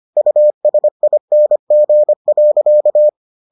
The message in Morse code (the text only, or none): USING.